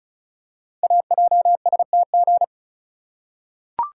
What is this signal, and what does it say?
Morse code, heard as a keyed tone, about 36 wpm, 690 Hz. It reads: AJSTG
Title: Code Groups - 5 Letters - 35wpm